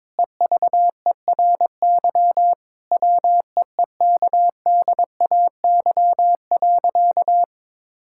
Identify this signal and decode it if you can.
Morse code, heard as a keyed tone, about 22 wpm, 700 Hz. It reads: EVERY WEEKDAY.
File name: Difficult Sentences with Hints - 22wpm